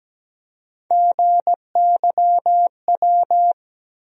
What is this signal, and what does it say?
Morse code, heard as a keyed tone, about 17 wpm, 700 Hz. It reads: GYW
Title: USA Call Sign Postfixes - 3 Letters 17wpm